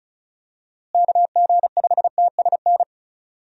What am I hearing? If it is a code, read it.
Morse code, heard as a keyed tone, about 34 wpm, 700 Hz. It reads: KG5TSN
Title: USA Call Signs - 6 Characters - Repeated 2x - 35wpm